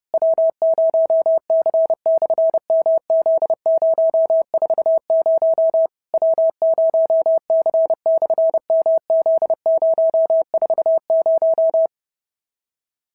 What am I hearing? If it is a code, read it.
Morse code, heard as a keyed tone, about 30 wpm, 650 Hz. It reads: W0C/MZ040 W0C/MZ040
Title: SOTA QSO Practice 30wpm